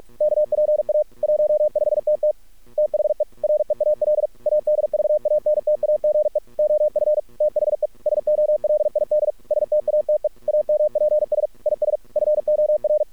Morse code, heard as a keyed tone, about 46 wpm, 600 Hz. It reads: KPA 15TT THE REAL ADVANTAGE OF THE SOLID STATE AMPS IS FOR
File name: archived / CRSnet